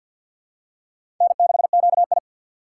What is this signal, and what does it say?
Morse code, heard as a keyed tone, about 53 wpm, 700 Hz. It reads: N6QI
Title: USA Call Signs - 4 Characters - 3x Character Spacing - 50wpm